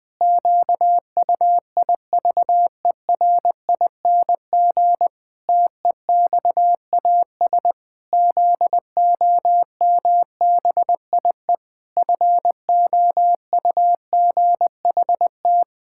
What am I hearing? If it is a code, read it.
Morse code, heard as a keyed tone, about 20 wpm, 700 Hz. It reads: QUIVERING TEXAS ZOMBIE FOUGHT
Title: Pangram Challenge v2 - 20wpm